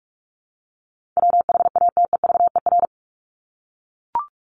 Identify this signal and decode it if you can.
Morse code, heard as a keyed tone, about 44 wpm, 700 Hz. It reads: WHATEVER
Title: Top 1001-1250 Words 45wpm